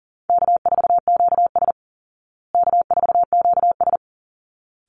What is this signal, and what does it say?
Morse code, heard as a keyed tone, about 40 wpm, 700 Hz. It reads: K4QS K4QS
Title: Easy Call Signs Repeated 2x - 40wpm